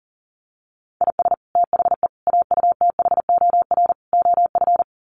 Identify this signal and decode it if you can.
Morse code, heard as a keyed tone, about 40 wpm, 700 Hz. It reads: IS THE AUTHOR OF
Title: Sets of 4 Words 40wpm